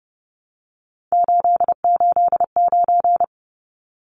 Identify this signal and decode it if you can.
Morse code, heard as a keyed tone, about 30 wpm, 700 Hz. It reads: 889
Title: Numbers - 3-Digit 30wpm